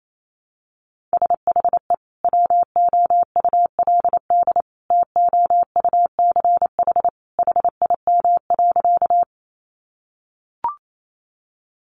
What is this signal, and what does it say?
Morse code, heard as a keyed tone, about 28 wpm, 700 Hz. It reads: SHE WOULD TOUCH HIM.
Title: Sentences from Top 1200 Words 28wpm